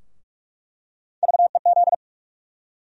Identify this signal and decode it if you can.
Morse code, heard as a keyed tone, about 46 wpm, 700 Hz. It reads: VEZ
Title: USA Call Sign Postfixes - 3 Letters 45wpm